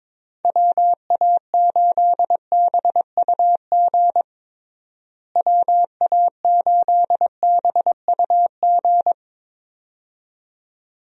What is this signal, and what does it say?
Morse code, heard as a keyed tone, about 22 wpm, 700 Hz. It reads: WA8BUG WA8BUG
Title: USA Call Signs - 6 Characters - Repeated 2x - 22wpm